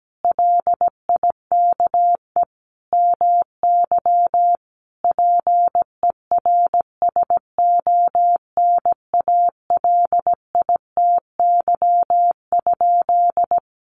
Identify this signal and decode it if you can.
Morse code, heard as a keyed tone, about 17 wpm, 700 Hz. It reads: LIKE MY PERSONALITY?
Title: Sentences from Top 1400 Words 17wpm